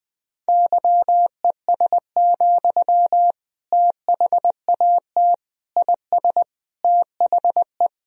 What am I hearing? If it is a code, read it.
Morse code, heard as a keyed tone, about 21 wpm, 700 Hz. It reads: YES, THAT IS THE
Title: Sentences from Top 1000 Words - Review Part 4 - 20wpm